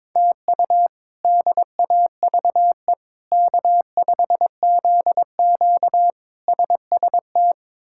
Morse code, heard as a keyed tone, about 21 wpm, 700 Hz. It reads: TU DAVE K5ZQ SST
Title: K1USN SST QSO Practice - 22wpm